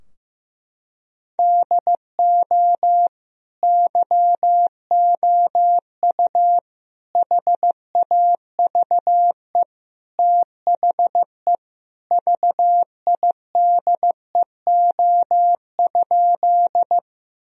Morse code, heard as a keyed tone, about 16 wpm, 700 Hz. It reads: DO YOU HAVE THE VIDEO?
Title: Sentences from Top 1000 Words - Review Part 4 - 15wpm